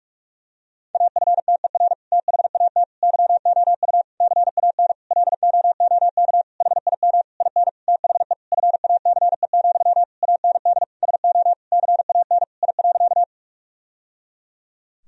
Morse code, heard as a keyed tone, about 47 wpm, 700 Hz. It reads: AFTER THAT YOU CAN LOOK HIM IN THE FACE, AND SO CAN I.
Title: Sentences from Top 1000 Words - Review Part 2 - 45wpm